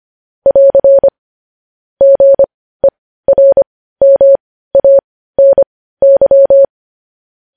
Morse code, heard as a keyed tone, about 25 wpm, spaced as for 15 wpm, 550 Hz. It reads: <AR> GERMANY